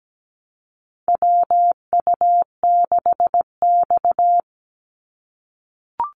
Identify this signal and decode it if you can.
Morse code, heard as a keyed tone, about 17 wpm, 700 Hz. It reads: WU6X
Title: Call Signs - 17wpm